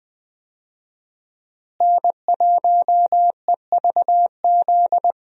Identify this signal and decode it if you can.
Morse code, heard as a keyed tone, about 20 wpm, 700 Hz. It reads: N1EVZ
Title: USA Call Signs - 5 Characters - Repeated 3x - 20wpm